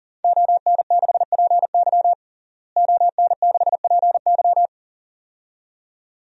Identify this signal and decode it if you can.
Morse code, heard as a keyed tone, about 37 wpm, 700 Hz. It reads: ON6PY ON6PY
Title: DXCC-100 Call Signs - 5 Characters - Repeated 2x - 40wpm